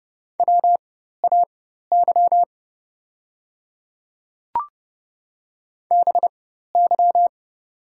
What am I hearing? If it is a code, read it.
Morse code, heard as a keyed tone, about 30 wpm, spaced as for 14 wpm, 700 Hz. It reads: WAY BY
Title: 4x Character Spacing - Top 100 Words - 30wpm